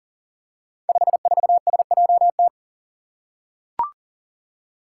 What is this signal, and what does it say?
Morse code, heard as a keyed tone, about 42 wpm, 700 Hz. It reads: 54SJT
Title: Five Challenging Letter-Number Combinations 40wpm